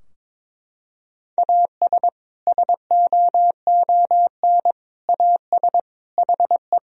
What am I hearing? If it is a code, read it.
Morse code, heard as a keyed tone, about 22 wpm, 700 Hz. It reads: AS SOON AS HE
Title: Sentences from Top 1000 Words - Review Part 2 - 22wpm